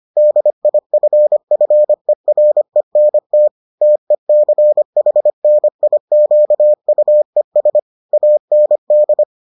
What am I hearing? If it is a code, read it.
Morse code, heard as a keyed tone, about 25 wpm, 600 Hz. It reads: DIFFERENT TECHNIQUES AND